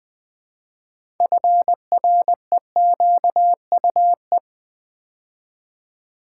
Morse code, heard as a keyed tone, about 20 wpm, 700 Hz. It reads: FREQUE
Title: Build A Word - 3 Letters At A Time - 9-Letter Words - 20wpm